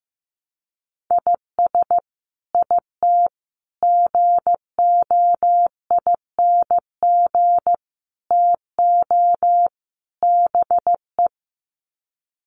Morse code, heard as a keyed tone, about 15 wpm, 700 Hz. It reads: IS IT GOING TO BE
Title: Sets of 5 Words Encore2 15wpm